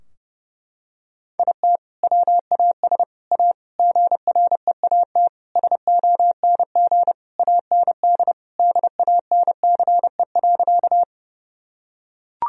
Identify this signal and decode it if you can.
Morse code, heard as a keyed tone, about 30 wpm, 700 Hz. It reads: IT WAS A GREAT SONG AND DANCE.
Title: Sentences from Top 1000 Words - Review Part 4 - 30wpm